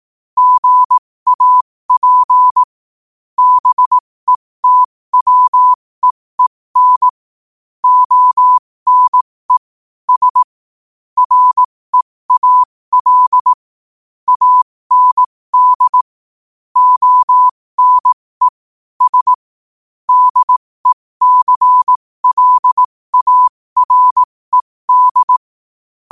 Morse code, heard as a keyed tone, about 18 wpm, spaced as for 15 wpm, 1000 Hz. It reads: GAP BETWEEN ONE S REAL AND ONE S DECLARED
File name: qotdfiles / 15wpm